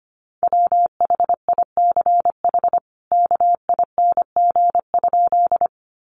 Morse code, heard as a keyed tone, about 25 wpm, 700 Hz. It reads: WHICH KING?
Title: Sentences from Top 400 Words 25wpm